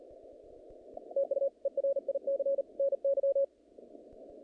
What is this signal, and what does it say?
Morse code, heard as a keyed tone, about 40 wpm, 550 Hz. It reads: TU ERIC NY
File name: contests / naqp-cw-k5tr-op-k5pi- -2200